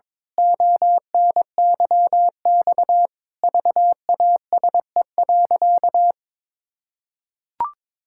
Morse code, heard as a keyed tone, about 22 wpm, 700 Hz. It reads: ONYX VASE.
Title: Pangram Challenge v2 - 22wpm